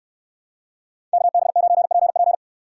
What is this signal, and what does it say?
Morse code, heard as a keyed tone, about 69 wpm, 700 Hz. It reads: KB9PP